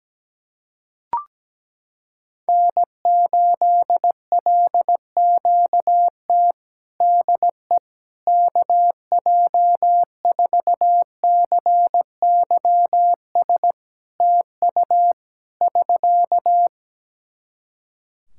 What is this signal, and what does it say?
Morse code, heard as a keyed tone, about 17 wpm, 700 Hz. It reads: N8LQT DE KJ4CYS TU <SK>
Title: K1USN SST QSO Practice 17wpm